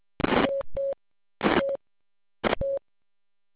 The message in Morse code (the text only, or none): none